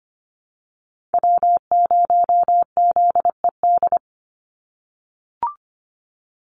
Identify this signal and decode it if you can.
Morse code, heard as a keyed tone, about 25 wpm, 700 Hz. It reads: W0ZED
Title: USA Call Signs 25wpm